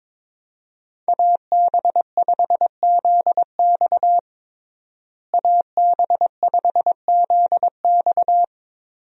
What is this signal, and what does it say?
Morse code, heard as a keyed tone, about 22 wpm, 700 Hz. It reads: AB5ZX AB5ZX
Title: USA Call Signs - 5 Characters - Repeated 3x - 22wpm